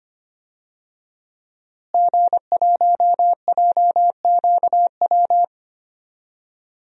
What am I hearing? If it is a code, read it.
Morse code, heard as a keyed tone, about 25 wpm, 700 Hz. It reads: G1JQW
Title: DXCC-100 Call Signs - 5 Characters - Repeated 3x - 25wpm